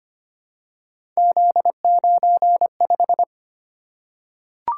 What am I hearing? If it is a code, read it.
Morse code, heard as a keyed tone, about 25 wpm, 700 Hz. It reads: Z95
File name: Three Letter-Number Combinations - 25wpm